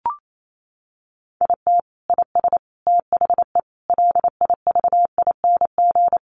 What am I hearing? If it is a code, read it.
Morse code, heard as a keyed tone, about 28 wpm, 700 Hz. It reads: IT IS THE LIVING
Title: Sentences from Top 1100 Words 28wpm